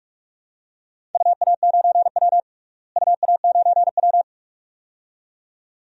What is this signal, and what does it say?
Morse code, heard as a keyed tone, about 43 wpm, 700 Hz. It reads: UA9W UA9W
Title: DXCC-100 Call Signs - 4 Characters - Repeated 2x - 45wpm